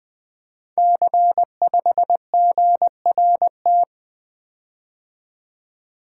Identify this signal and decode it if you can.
Morse code, heard as a keyed tone, about 20 wpm, 700 Hz. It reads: C5GRT
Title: Code Groups - 5 Letter:Numbers - 20wpm